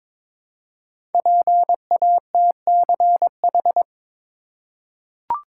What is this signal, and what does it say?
Morse code, heard as a keyed tone, about 22 wpm, 700 Hz. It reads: PATCH
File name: 5 Letter Words - First 3 Letters Given - 22wpm